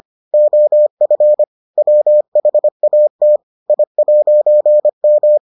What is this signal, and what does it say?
Morse code, heard as a keyed tone, about 25 wpm, 600 Hz. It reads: OF WHAT I'M